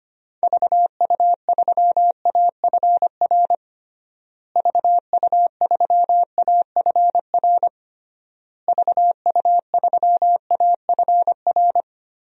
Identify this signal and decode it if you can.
Morse code, heard as a keyed tone, about 25 wpm, 700 Hz. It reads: VU3AFR VU3AFR VU3AFR
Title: DXCC-100 Call Signs - 6 Characters - Repeated 3x - 25wpm